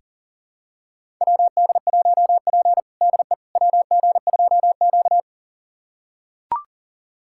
Morse code, heard as a keyed tone, about 40 wpm, 700 Hz. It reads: WD1P DE WG2Q
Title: USA Callsign DE Callsign - 4 Characters - 40wpm